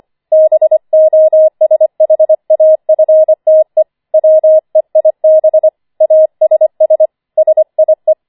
Morse code, heard as a keyed tone, about 25 wpm, 630 Hz. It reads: BOSHAFTE WEIB ASS SIE
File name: 25wpm